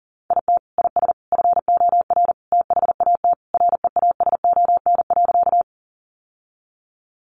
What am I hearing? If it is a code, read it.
Morse code, heard as a keyed tone, about 37 wpm, 700 Hz. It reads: IT IS FOR THAT REASON.
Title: Sentences from Top 700 Words 40wpm